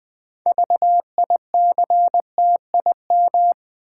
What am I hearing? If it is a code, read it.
Morse code, heard as a keyed tone, about 20 wpm, 700 Hz. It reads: VICTIM
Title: 6 Letter Words - First 3 Letters Given - 20wpm